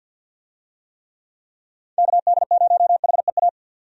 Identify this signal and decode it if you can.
Morse code, heard as a keyed tone, about 50 wpm, 700 Hz. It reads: KD0HEA
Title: USA Call Signs - 6 Characters - 50wpm